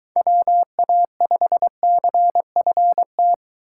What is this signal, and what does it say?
Morse code, heard as a keyed tone, about 23 wpm, 700 Hz. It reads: WA5CFT